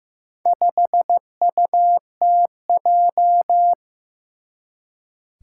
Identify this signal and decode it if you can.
Morse code, heard as a keyed tone, about 15 wpm, 700 Hz. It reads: 5UTJ
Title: Four Challenging Letter-Number Combinations 15wpm